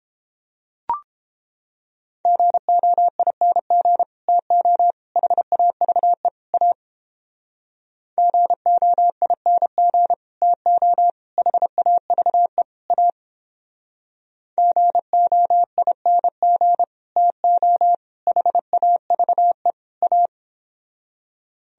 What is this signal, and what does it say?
Morse code, heard as a keyed tone, about 31 wpm, 700 Hz. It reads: GOING TO HAVE A GOING TO HAVE A GOING TO HAVE A